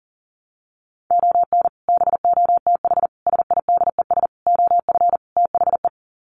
Morse code, heard as a keyed tone, about 40 wpm, 700 Hz. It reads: ON BOTH SIDES OF THE
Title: Sets of 5 Words 40wpm